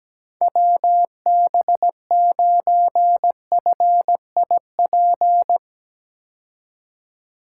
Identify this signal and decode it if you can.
Morse code, heard as a keyed tone, about 17 wpm, 700 Hz. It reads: WB9FIP
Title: K1USN SST QSO Practice - 17wpm